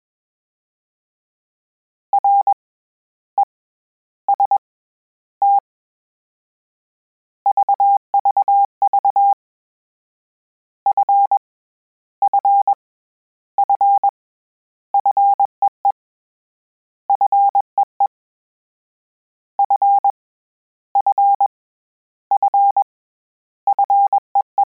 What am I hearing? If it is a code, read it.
Morse code, heard as a keyed tone, about 21 wpm, 800 Hz. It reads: R E S T VVV F F F FEE FEE F F F FEE
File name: Combinations / Combinations-mp3s / Combinations@20